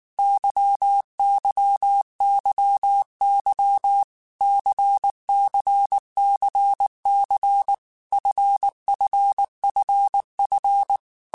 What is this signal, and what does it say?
Morse code, heard as a keyed tone, about 19 wpm, 800 Hz. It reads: YYYY CCCC FFFF
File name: Groups@18 / Groups@18 mp3 / LFCYQ@18